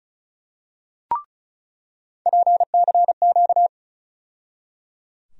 Morse code, heard as a keyed tone, about 34 wpm, 700 Hz. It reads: PCQ
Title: USA Call Sign Postfixes - 3 Letters 35wpm